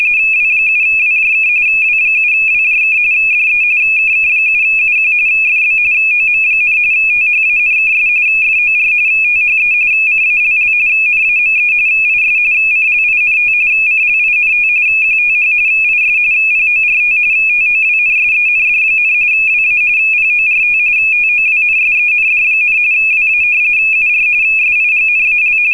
RTTY, about 45.45 baud. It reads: W7WJK W7WJK DE KK7UQ FB OM, THANKS FOR THE CALL NAME IS CLINT CLINT QTH IS PORT TOWNSEND, WA PORT TOWNSEND, WA UR SIGNAL